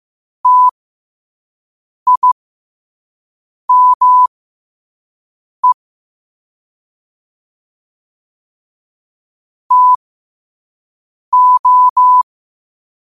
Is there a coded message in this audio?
Yes: Morse code from 0.5 to 12.2 s, heard as a keyed tone, about 15 wpm, spaced as for 5 wpm, 1000 Hz. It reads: TIME TO